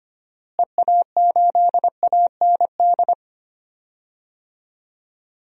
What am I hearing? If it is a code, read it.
Morse code, heard as a keyed tone, about 25 wpm, 700 Hz. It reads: EA8AND
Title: DXCC-100 Call Signs - 6 Characters - Repeated 2x - 25wpm